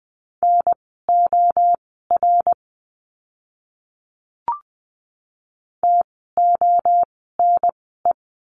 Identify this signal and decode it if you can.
Morse code, heard as a keyed tone, about 20 wpm, spaced as for 15 wpm, 700 Hz. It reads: NOR TONE
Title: 068 Lesson 12f - Letter d - Words 20wpm